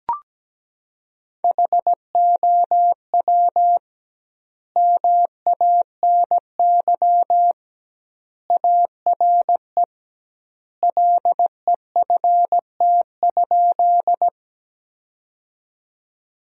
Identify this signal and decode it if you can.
Morse code, heard as a keyed tone, about 17 wpm, 700 Hz. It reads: HOW MANY ARE LEFT?
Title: 2x Word-Spacing - Sentences from Top 300 Words - 17wpm